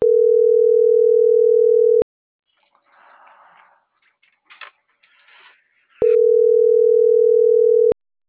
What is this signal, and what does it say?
Ringback tone (North America), 440 + 480 Hz.